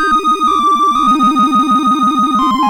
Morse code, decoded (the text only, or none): none